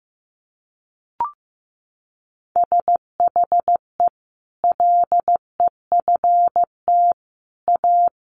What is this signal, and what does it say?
Morse code, heard as a keyed tone, about 15 wpm, 700 Hz. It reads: SHE LEFT A
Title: Sentences from Top 1100 Words 15wpm